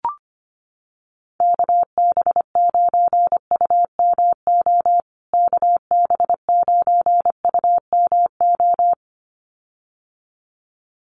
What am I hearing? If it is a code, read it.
Morse code, heard as a keyed tone, about 25 wpm, 700 Hz. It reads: KB9UMO KB9UMO
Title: USA Call Signs Repeated 2x 25wpm